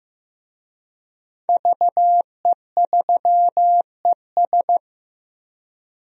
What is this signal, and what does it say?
Morse code, heard as a keyed tone, about 15 wpm, 700 Hz. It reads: VE3ES
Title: DXCC-100 Call Signs - 5 Characters - 15wpm